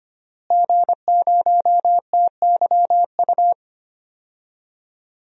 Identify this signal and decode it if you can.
Morse code, heard as a keyed tone, about 25 wpm, 700 Hz. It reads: G0TYU
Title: DXCC-100 Call Signs - 5 Characters - Repeated 2x - 25wpm